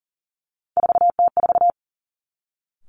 Morse code, heard as a keyed tone, about 41 wpm, 700 Hz. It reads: 4T4
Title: Three Challenging Letter-Number Combinations 40wpm